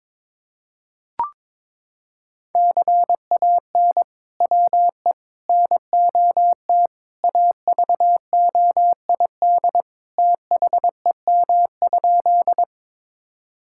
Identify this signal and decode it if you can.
Morse code, heard as a keyed tone, about 22 wpm, 700 Hz. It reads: CAN WE NOT AVOID THEM?
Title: Sentences from Top 1000 Words - Review Part 6 - 22wpm